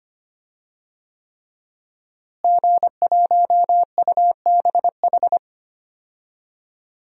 Morse code, heard as a keyed tone, about 25 wpm, 700 Hz. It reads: G1UBH
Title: DXCC-100 Call Signs - 5 Characters - Repeated 3x - 25wpm